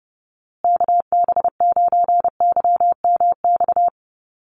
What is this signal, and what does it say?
Morse code, heard as a keyed tone, about 30 wpm, 700 Hz. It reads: KB9YMX
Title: USA Call Signs Repeated 3x 30wpm